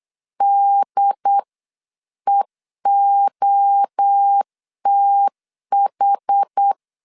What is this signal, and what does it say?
Morse code, heard as a keyed tone, about 8 wpm, 800 Hz. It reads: DEOTH
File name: CW / DET / 1 3 1500C S C 8wpm 4214